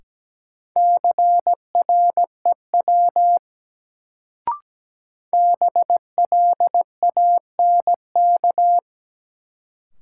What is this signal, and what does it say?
Morse code, heard as a keyed tone, about 17 wpm, 700 Hz. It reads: CREW BLANK
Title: Sentences from Top 1500 Words - New Word Drill Alt - 17wpm